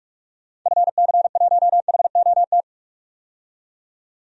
Morse code, heard as a keyed tone, about 46 wpm, 700 Hz. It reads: FC1HOT
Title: DXCC-100 Call Signs - 6 Characters - 45wpm